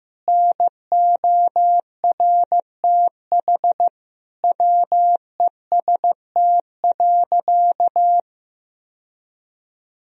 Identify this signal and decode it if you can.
Morse code, heard as a keyed tone, about 16 wpm, 700 Hz. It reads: NORTH WEST.
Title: Sentences from Top 1000 Words - Review Part 2 - 15wpm